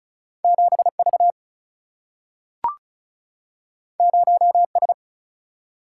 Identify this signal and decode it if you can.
Morse code, heard as a keyed tone, about 36 wpm, 700 Hz. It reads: 7V 0S
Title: Single-Letter Single-Number 35wpm